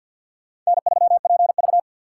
Morse code, heard as a keyed tone, about 65 wpm, 700 Hz. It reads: N3PV